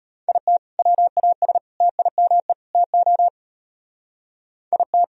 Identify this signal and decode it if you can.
Morse code, heard as a keyed tone, about 34 wpm, 700 Hz. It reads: IT WAS TIME TO IT